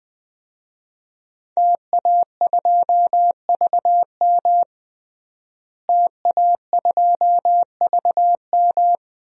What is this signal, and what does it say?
Morse code, heard as a keyed tone, about 21 wpm, 700 Hz. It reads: TA2VM TA2VM
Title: DXCC-100 Call Signs - 5 Characters - Repeated 3x - 20wpm